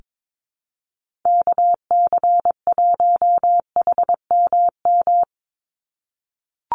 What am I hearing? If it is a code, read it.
Morse code, heard as a keyed tone, about 22 wpm, 700 Hz. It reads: KC1HMM
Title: USA Call Signs 22wpm